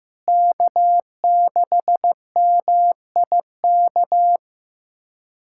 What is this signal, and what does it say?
Morse code, heard as a keyed tone, about 15 wpm, 700 Hz. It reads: K6MIK
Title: USA Callsign DE Callsign - 5 Characters - 15wpm